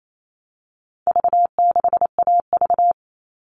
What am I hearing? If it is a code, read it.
Morse code, heard as a keyed tone, about 28 wpm, 700 Hz. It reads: V6AV
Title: All Cut Numbers - 4-Digit 28wpm